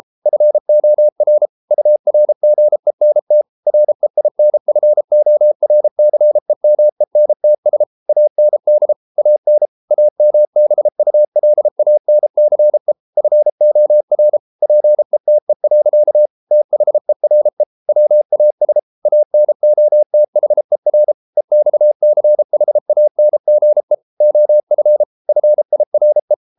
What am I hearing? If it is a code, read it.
Morse code, heard as a keyed tone, about 33 wpm, 600 Hz. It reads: FOR URGENT REINFORCEMENTS AND AN AMBULANCE FOR PETE. THERE WAS ANOTHER EXCHANGE OF FIRE